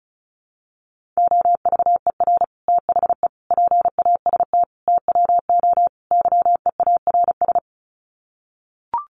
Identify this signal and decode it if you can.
Morse code, heard as a keyed tone, about 34 wpm, 700 Hz. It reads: OVER THE PAST TWO YEARS
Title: Sets of 5 Words Encore 35wpm